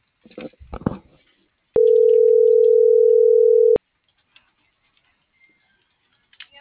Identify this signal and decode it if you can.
Ringback tone (North America), 440 + 480 Hz.